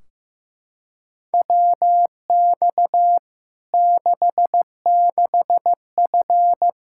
Morse code, heard as a keyed tone, about 15 wpm, 700 Hz. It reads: WX 66F
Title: Practice QSO - Exchange 2 - Info - 15wpm